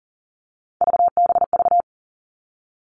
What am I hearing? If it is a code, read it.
Morse code, heard as a keyed tone, about 40 wpm, 700 Hz. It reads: VBV
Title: All Cut Numbers - 3-Digit 40wpm